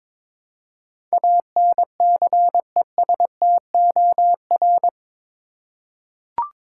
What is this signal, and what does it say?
Morse code, heard as a keyed tone, about 22 wpm, 700 Hz. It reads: ANCESTOR
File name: practice / 22wpm / Eight Letter Words - 22wpm